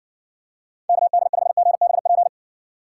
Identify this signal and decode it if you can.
Morse code, heard as a keyed tone, about 55 wpm, 700 Hz. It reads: KD5GBP